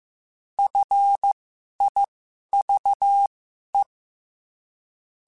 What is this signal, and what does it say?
Morse code, heard as a keyed tone, about 15 wpm, spaced as for 11 wpm, 800 Hz. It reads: FIVE